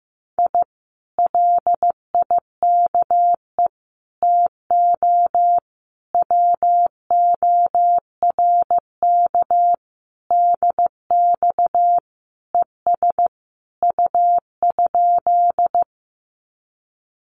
Morse code, heard as a keyed tone, about 15 wpm, 700 Hz. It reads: I LIKE TO WORK DX ES U?
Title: Common QSO Phrases Encore 15wpm